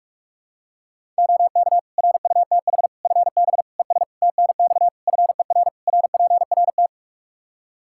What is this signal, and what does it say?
Morse code, heard as a keyed tone, about 41 wpm, 700 Hz. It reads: OK RUTH FB ES TNX FER RPRT
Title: Practice QSO - Exchange 2 - Info - 45wpm